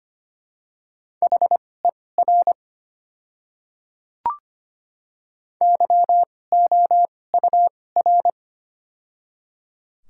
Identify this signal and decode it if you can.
Morse code, heard as a keyed tone, about 25 wpm, spaced as for 18 wpm, 700 Hz. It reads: HER YOUR
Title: 2x Character Spacing - Top 100 Words - 25wpm